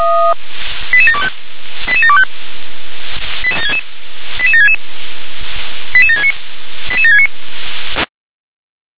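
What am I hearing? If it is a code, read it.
Five-tone selective call (ZVEI), heard as quick runs of five tones, tones of about 70 ms. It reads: 88725 88725 88766 88769 88769
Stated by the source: ZVEI five-tone selective call 88769 received on SDR.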